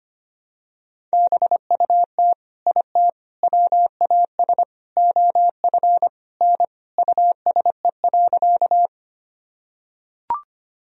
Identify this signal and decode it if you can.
Morse code, heard as a keyed tone, about 25 wpm, 700 Hz. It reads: BUT IT WAS OF N USE.
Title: Sentences from Top 100 Words and Abbreviations 25wpm